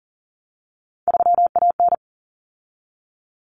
Morse code, heard as a keyed tone, about 40 wpm, 700 Hz. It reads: 3AN
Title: Common Cut Numbers - 3-Digit 40wpm